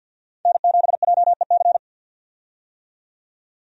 Morse code, heard as a keyed tone, about 47 wpm, 700 Hz. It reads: N7JEC
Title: USA Call Signs - 5 Characters - 50wpm